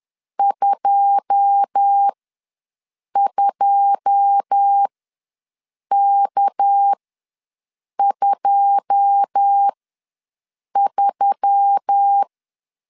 Morse code, heard as a keyed tone, about 10 wpm, spaced as for 6 wpm, 800 Hz. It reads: 22K23
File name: LEZIONI / 9 2 ws 10wpm 4130